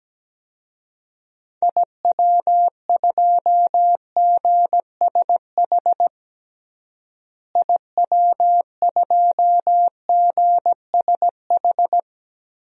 Morse code, heard as a keyed tone, about 17 wpm, 700 Hz. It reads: IW2GSH IW2GSH
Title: DXCC-100 Call Signs - 6 Characters - Repeated 3x - 17wpm